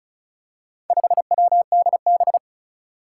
Morse code, heard as a keyed tone, about 36 wpm, 700 Hz. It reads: 5WDB
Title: Four Challenging Letter-Number Combinations 35wpm